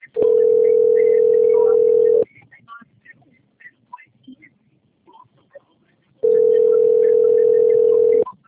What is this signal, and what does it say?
Ringback tone (North America), 440 + 480 Hz.